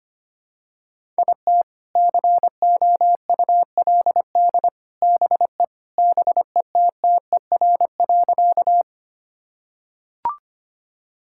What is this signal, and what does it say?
Morse code, heard as a keyed tone, about 25 wpm, 700 Hz. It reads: IT COULD BE BETTER.
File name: practice / 25wpm / Sentences with 2-Word Phrases - 25wpm